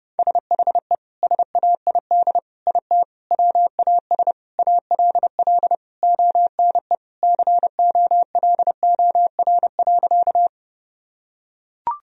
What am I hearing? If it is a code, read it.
Morse code, heard as a keyed tone, about 32 wpm, 700 Hz. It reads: SHE SAID IT WAS ALL ONE COLOR.
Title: Sentences from Top 1000 Words - Review Part 2 - 30wpm